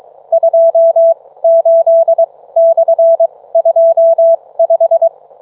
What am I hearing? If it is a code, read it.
Morse code, heard as a keyed tone, about 23 wpm, spaced as for 17 wpm, 660 Hz. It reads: 28/25